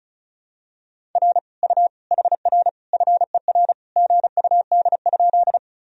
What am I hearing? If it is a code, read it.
Morse code, heard as a keyed tone, about 36 wpm, 700 Hz. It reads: R U HR FER GUD?
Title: Sentences from Top 200 Words and Abbreviations 35wpm